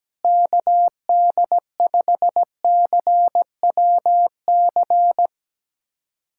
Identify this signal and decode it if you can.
Morse code, heard as a keyed tone, about 17 wpm, 700 Hz. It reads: KD5CWC
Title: USA Callsign DE Callsign 17wpm